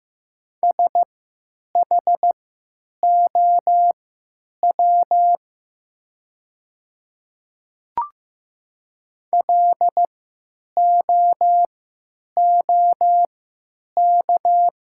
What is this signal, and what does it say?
Morse code, heard as a keyed tone, about 15 wpm, spaced as for 9 wpm, 700 Hz. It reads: SHOW LOOK
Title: 3x Character Spacing - Top 300 Words - 15wpm